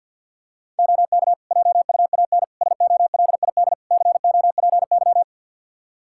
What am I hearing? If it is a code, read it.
Morse code, heard as a keyed tone, about 52 wpm, 700 Hz. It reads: OK JUAN SOLID COPY
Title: Practice QSO - Exchange 3 - More Info - 50wpm